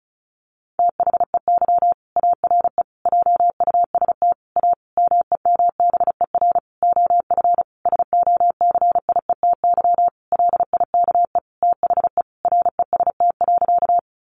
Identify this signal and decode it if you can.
Morse code, heard as a keyed tone, about 34 wpm, 700 Hz. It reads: THEY ARE JUST A MEMBER OF SOCIETY LIKE THE REST.
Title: Sentences from Top 1100 Words 35wpm